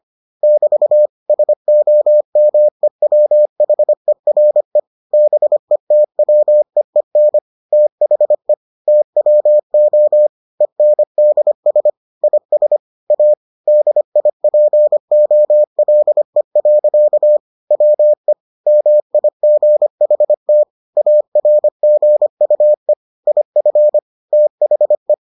Morse code, heard as a keyed tone, about 25 wpm, 600 Hz. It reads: <BT> SOMEWHERE BETWEEN THE TWO ENDS IS A DIPOLE. WE MIGHT ARGUE IF THE